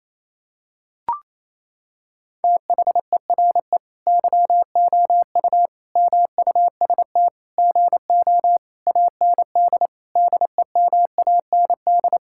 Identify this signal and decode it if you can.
Morse code, heard as a keyed tone, about 28 wpm, 700 Hz. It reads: THERE YOU MUST GO AND DEMAND
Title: Sentences from Top 1600 Words 28wpm